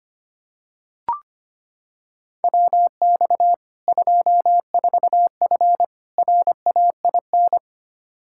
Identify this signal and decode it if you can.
Morse code, heard as a keyed tone, about 25 wpm, 700 Hz. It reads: WX 24F RAIN
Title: K1USN SST QSO Practice 25wpm